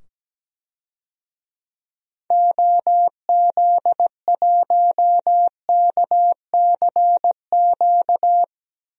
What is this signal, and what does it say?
Morse code, heard as a keyed tone, about 17 wpm, 700 Hz. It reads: OZ1KCQ
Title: DXCC-100 Call Signs - 6 Characters - 17wpm